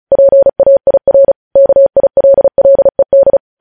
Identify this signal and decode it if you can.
Morse code, heard as a keyed tone, about 36 wpm, 550 Hz. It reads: PAIR KILLED
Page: news headlines in morse code at 35 wpm